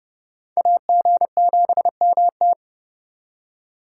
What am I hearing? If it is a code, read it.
Morse code, heard as a keyed tone, about 36 wpm, 700 Hz. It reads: AG7MT
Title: Speed Racing - USA 2x2 Call Signs 30wpm